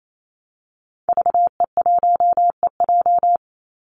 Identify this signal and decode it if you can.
Morse code, heard as a keyed tone, about 28 wpm, 700 Hz. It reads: VE1EJ
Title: Call Signs Repeated 3x - 28wpm